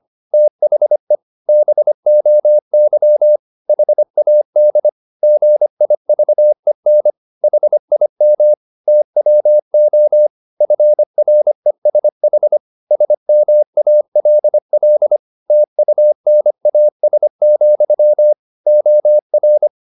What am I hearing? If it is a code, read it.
Morse code, heard as a keyed tone, about 25 wpm, 600 Hz. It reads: THE BOY HAD GIVEN HIM TWO FRESH SMALL TUNAS, OR
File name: the old man and the sea part 1 25wpm